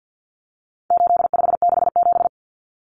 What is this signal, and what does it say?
Morse code, heard as a keyed tone, about 50 wpm, 700 Hz. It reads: 8567
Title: Numbers - 4-Digit 50wpm